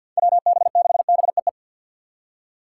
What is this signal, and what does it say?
Morse code, heard as a keyed tone, about 53 wpm, 700 Hz. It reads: WB6BEE
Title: K1USN SST QSO Practice - 50wpm